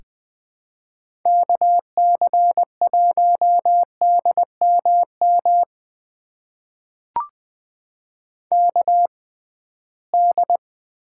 Morse code, heard as a keyed tone, about 20 wpm, spaced as for 15 wpm, 700 Hz. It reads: KC1DMM KD
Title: USA Call Signs - 6 Characters - 6x Character Spacing - 20wpm